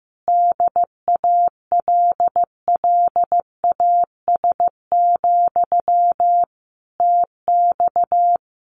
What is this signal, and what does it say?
Morse code, heard as a keyed tone, about 15 wpm, 700 Hz. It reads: DALLAS, TX
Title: Common QSO Phrases Encore 15wpm